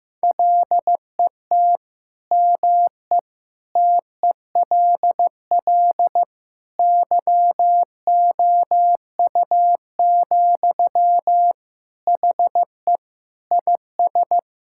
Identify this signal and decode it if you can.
Morse code, heard as a keyed tone, about 15 wpm, 700 Hz. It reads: LET ME TELL YOU, HE IS
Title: Sentences from Top 1600 Words 15wpm